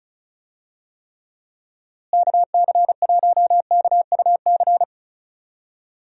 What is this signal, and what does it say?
Morse code, heard as a keyed tone, about 36 wpm, 700 Hz. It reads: KC1KUC
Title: USA Call Signs - 6 Characters - Repeated 3x - 35wpm